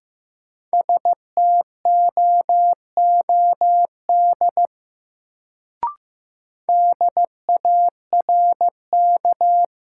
Morse code, heard as a keyed tone, about 15 wpm, 700 Hz. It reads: STOOD DARK
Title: Sentences from Top 500 Words - New Word Drill - 15wpm